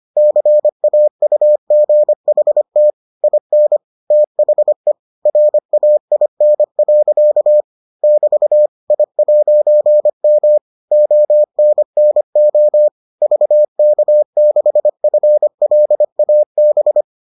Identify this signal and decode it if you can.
Morse code, heard as a keyed tone, about 25 wpm, 600 Hz. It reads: CAUGHT IN THE RAIN. <BT> I'M ONNO VK6FLAB